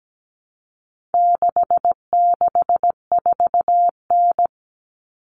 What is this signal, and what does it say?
Morse code, heard as a keyed tone, about 17 wpm, 700 Hz. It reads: 664N
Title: Common Cut Numbers - 4-Digit 17wpm